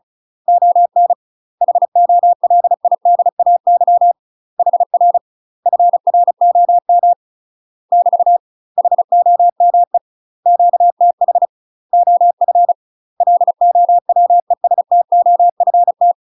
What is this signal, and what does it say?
Morse code, heard as a keyed tone, about 34 wpm, 700 Hz. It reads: ON HOLIDAY HR FROM <BT> HOME QTH OF LOWESTOFT
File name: UK MASTER QSOs TRAINING/35WPM